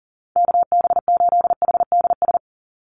Morse code, heard as a keyed tone, about 40 wpm, 700 Hz. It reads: KB8HDS
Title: USA Call Signs 40wpm